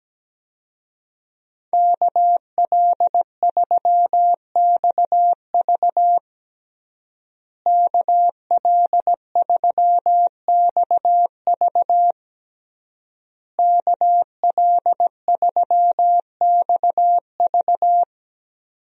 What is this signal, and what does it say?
Morse code, heard as a keyed tone, about 17 wpm, 700 Hz. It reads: KL3XV KL3XV KL3XV
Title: USA Call Signs - 5 Characters - Repeated 3x - 17wpm